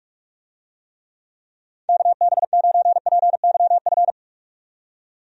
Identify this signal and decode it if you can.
Morse code, heard as a keyed tone, about 44 wpm, 700 Hz. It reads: KB9PYF
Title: USA Call Signs - 6 Characters - 45wpm